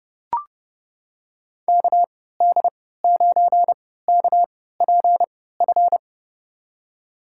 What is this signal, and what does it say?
Morse code, heard as a keyed tone, about 30 wpm, spaced as for 17 wpm, 700 Hz. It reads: KD9KPF
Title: USA Call Signs - 6 Characters - 3x Character Spacing - 30wpm